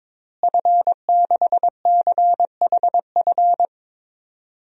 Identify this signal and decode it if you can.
Morse code, heard as a keyed tone, about 22 wpm, 700 Hz. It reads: F6CHF
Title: DXCC-100 Call Signs - 5 Characters - Repeated 3x - 22wpm